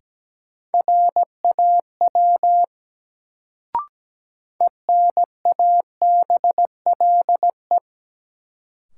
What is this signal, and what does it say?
Morse code, heard as a keyed tone, about 17 wpm, 700 Hz. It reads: RAW ENABLE
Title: Sentences from Top 1500 Words - New Word Drill - 17wpm